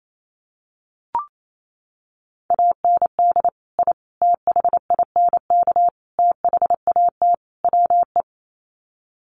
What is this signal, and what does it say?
Morse code, heard as a keyed tone, about 28 wpm, 700 Hz. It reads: AND I THINK THAT WE
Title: Sets of 5 Words Encore2 28wpm